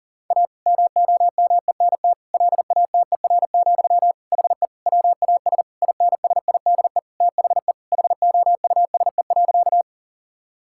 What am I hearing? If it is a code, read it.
Morse code, heard as a keyed tone, about 37 wpm, 700 Hz. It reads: A MOMENT LATER, HE WAS INSIDE THE HOUSE.
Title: Sentences from Top 1000 Words - Review Part 6 - 40wpm